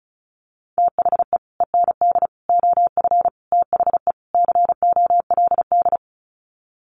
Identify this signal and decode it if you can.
Morse code, heard as a keyed tone, about 34 wpm, 700 Hz. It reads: THE END OF THE COLD
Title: Sets of 5 Words 35wpm